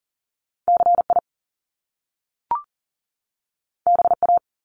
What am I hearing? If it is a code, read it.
Morse code, heard as a keyed tone, about 42 wpm, 700 Hz. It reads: CI BA
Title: Top Two Letter Combinations in Words 40wpm